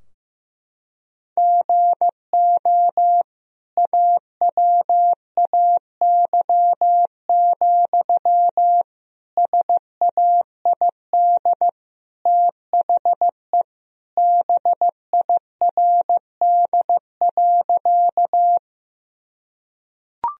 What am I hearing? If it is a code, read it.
Morse code, heard as a keyed tone, about 15 wpm, 700 Hz. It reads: GO AWAY, SAID THE BIRD.
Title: Sentences from Top 1000 Words - Review Part 2 - 15wpm